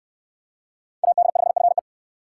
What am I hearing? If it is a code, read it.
Morse code, heard as a keyed tone, about 69 wpm, 700 Hz. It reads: AD5FE